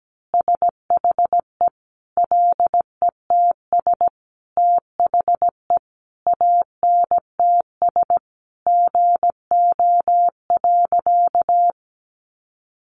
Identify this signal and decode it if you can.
Morse code, heard as a keyed tone, about 17 wpm, 700 Hz. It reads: SHE LETS THE ANTS GO.
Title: Additional Easy Phrases 17wpm